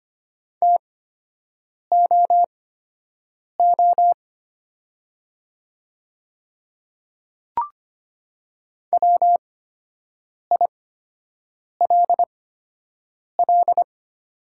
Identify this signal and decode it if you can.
Morse code, heard as a keyed tone, about 25 wpm, spaced as for 7 wpm, 700 Hz. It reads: TOO WILL
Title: 8x Character Spacing - Top 300 Words - 25wpm